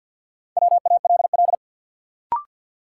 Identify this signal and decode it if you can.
Morse code, heard as a keyed tone, about 52 wpm, 700 Hz. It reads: WALL
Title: Sentences from Top 1100 Words - New Word Drill Alt - 50wpm